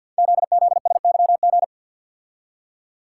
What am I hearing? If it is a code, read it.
Morse code, heard as a keyed tone, about 47 wpm, 700 Hz. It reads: ZZSYG
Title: Code Groups - 5 Letters - 50wpm